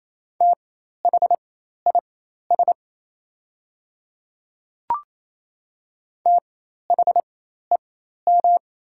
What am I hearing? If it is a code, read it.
Morse code, heard as a keyed tone, about 27 wpm, spaced as for 13 wpm, 690 Hz. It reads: THIS THEM
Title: 4x Character Spacing - Top 100 Words - 28wpm